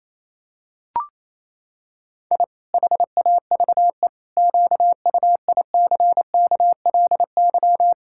Morse code, heard as a keyed tone, about 28 wpm, 700 Hz. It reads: I HAVE QUICKLY
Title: Pangram Challenge v2 - 28wpm